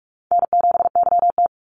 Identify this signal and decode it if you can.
Morse code, heard as a keyed tone, about 44 wpm, 700 Hz. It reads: N7YT
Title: Call Signs Repeated 3x - 45wpm